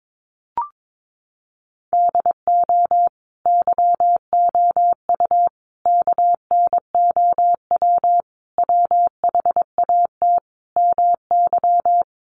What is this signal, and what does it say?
Morse code, heard as a keyed tone, about 22 wpm, 700 Hz. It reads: DO YOU KNOW WHAT MY
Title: Sentences from Top 1200 Words 22wpm